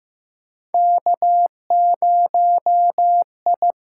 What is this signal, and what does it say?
Morse code, heard as a keyed tone, about 15 wpm, 700 Hz. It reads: K0I
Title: USA Call Signs - 6x Character Spacing - 15wpm